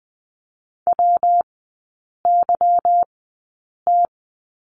Morse code, heard as a keyed tone, about 20 wpm, spaced as for 8 wpm, 700 Hz. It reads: WYT
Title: 152 Lesson 25e - Letter p - Three Characters 20wpm